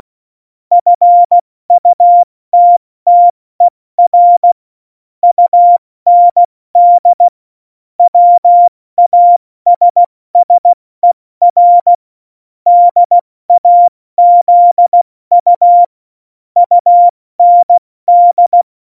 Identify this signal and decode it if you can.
Morse code, heard as a keyed tone, about 16 wpm, 700 Hz. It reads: FUTTER UND WASSER DAZU UND
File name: Morsekurs / Witze / Witz 08. Januar 2019, 70 F80